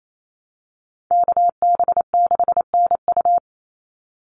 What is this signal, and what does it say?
Morse code, heard as a keyed tone, about 28 wpm, 700 Hz. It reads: KB6NU
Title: Call Signs Repeated 3x - 28wpm